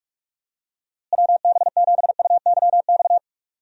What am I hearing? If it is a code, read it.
Morse code, heard as a keyed tone, about 41 wpm, 700 Hz. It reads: WB7UYX
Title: USA Call Signs - 6 Characters - Repeated 3x - 45wpm